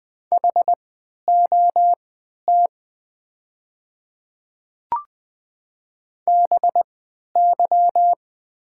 Morse code, heard as a keyed tone, about 20 wpm, spaced as for 11 wpm, 700 Hz. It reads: HOT BY
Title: 3x Character Spacing - Top 200 Words - 20wpm